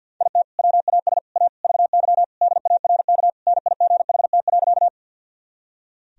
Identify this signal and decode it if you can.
Morse code, heard as a keyed tone, about 49 wpm, 700 Hz. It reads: IT WAS A VY DARK NIGHT.
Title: Sentences from Top 500 Words and Abbreviations 50wpm